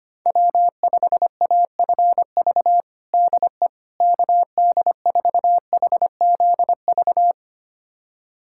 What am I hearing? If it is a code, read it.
Morse code, heard as a keyed tone, about 25 wpm, 700 Hz. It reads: W5AFV DE KD4HZV
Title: USA Callsign DE Callsign 25wpm